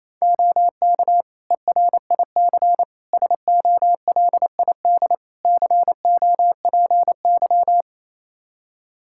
Morse code, heard as a keyed tone, about 28 wpm, 700 Hz. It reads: OK ERIC SOLID COPY
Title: Practice QSO - Exchange 3 - More Info - 28wpm